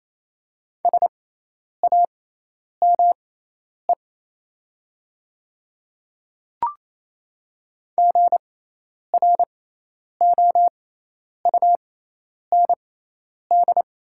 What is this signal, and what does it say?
Morse code, heard as a keyed tone, about 28 wpm, spaced as for 10 wpm, 700 Hz. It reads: SAME GROUND
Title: 6x Character Spacing - Top 500 Words - 28wpm